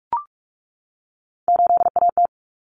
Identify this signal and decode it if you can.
Morse code, heard as a keyed tone, about 44 wpm, 700 Hz. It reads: 8AT
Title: Common Cut Numbers - 3-Digit 45wpm